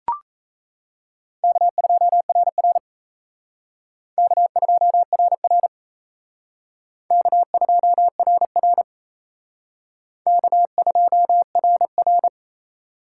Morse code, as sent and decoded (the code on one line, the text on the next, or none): -.- ..--- .-. .-. / -.- ..--- .-. .-. / -.- ..--- .-. .-. / -.- ..--- .-. .-.
K2RR K2RR K2RR K2RR